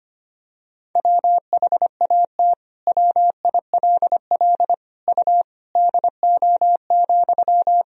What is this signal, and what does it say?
Morse code, heard as a keyed tone, about 25 wpm, 700 Hz. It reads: WHAT WILL U DO,
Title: Sentences from Top 100 Words and Abbreviations 25wpm